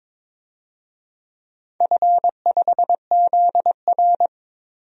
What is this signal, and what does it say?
Morse code, heard as a keyed tone, about 22 wpm, 700 Hz. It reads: F5ZR
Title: DXCC-100 Call Signs - 4 Characters - 22wpm